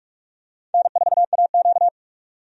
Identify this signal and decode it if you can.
Morse code, heard as a keyed tone, about 46 wpm, 700 Hz. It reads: N4AQ
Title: USA Call Signs - 4 Characters - 45wpm